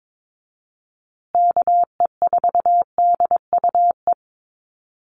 Morse code, heard as a keyed tone, about 22 wpm, 700 Hz. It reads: KE4DUE
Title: USA Call Signs Repeated 2x 22wpm